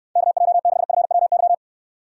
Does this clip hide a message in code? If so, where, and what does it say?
Yes: Morse code from 0.2 to 1.5 s, heard as a keyed tone, about 69 wpm, 700 Hz. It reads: KJ6FWY